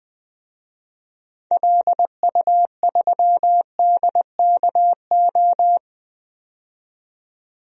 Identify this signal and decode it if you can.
Morse code, heard as a keyed tone, about 21 wpm, 700 Hz. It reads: LU3DKO
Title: DXCC-100 Call Signs - 6 Characters - 20wpm